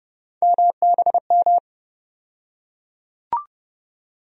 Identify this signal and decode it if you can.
Morse code, heard as a keyed tone, about 30 wpm, 700 Hz. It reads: MBM
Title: USA Call Sign Postfixes - 3 Letters 30wpm